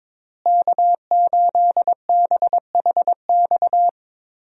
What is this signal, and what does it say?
Morse code, heard as a keyed tone, about 22 wpm, 700 Hz. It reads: K8BHX
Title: USA Call Signs - 5 Characters - Repeated 3x - 22wpm